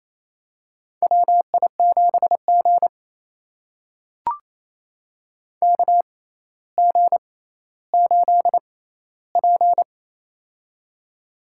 Morse code, heard as a keyed tone, about 28 wpm, spaced as for 14 wpm, 700 Hz. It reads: WI7G KG8P
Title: USA Call Signs - 6x Character Spacing - 28wpm